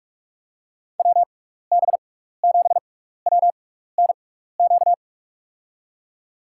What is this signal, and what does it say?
Morse code, heard as a keyed tone, about 46 wpm, spaced as for 16 wpm, 700 Hz. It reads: WB7WNQ
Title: USA Call Signs - 6x Character Spacing - 45wpm